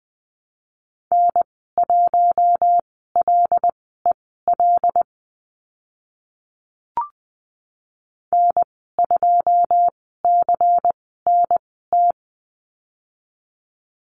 Morse code, heard as a keyed tone, about 20 wpm, spaced as for 14 wpm, 700 Hz. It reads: N1LEL N2CNT
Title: 140 Lesson 23g - Letter f - Call Signs 20wpm